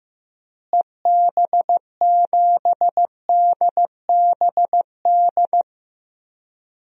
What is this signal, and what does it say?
Morse code, heard as a keyed tone, about 15 wpm, 700 Hz. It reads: EB7DBD
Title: DXCC-100 Call Signs - 6 Characters - Repeated 3x - 15wpm